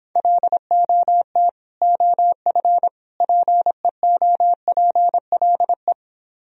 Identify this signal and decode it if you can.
Morse code, heard as a keyed tone, about 26 wpm, 700 Hz. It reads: LOT OF PEOPLE